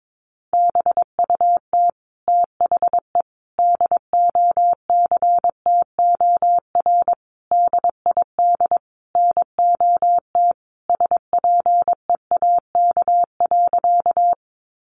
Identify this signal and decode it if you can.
Morse code, heard as a keyed tone, about 22 wpm, 700 Hz. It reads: BUT THE DOCTOR DID NOT SPEAK.
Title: Sentences from Top 1500 Words 22wpm